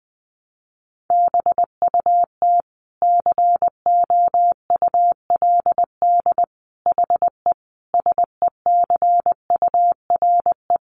Morse code, heard as a keyed tone, about 20 wpm, 700 Hz. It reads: BUT COULD HE SECURE
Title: Sentences from Top 1200 Words 20wpm